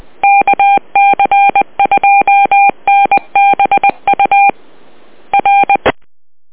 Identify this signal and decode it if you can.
Morse code, heard as a keyed tone, about 20 wpm, 800 Hz. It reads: KC2NBU R